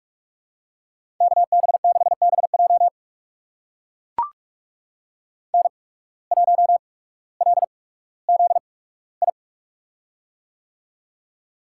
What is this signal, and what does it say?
Morse code, heard as a keyed tone, about 45 wpm, spaced as for 19 wpm, 700 Hz. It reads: KB6BJ N1LZI
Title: USA Call Signs - 5 Characters - 8x Character Spacing - 45wpm